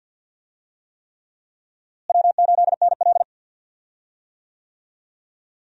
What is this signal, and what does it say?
Morse code, heard as a keyed tone, about 47 wpm, 700 Hz. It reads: W8NL
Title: USA Call Signs - 4 Characters - Repeated 3x - 50wpm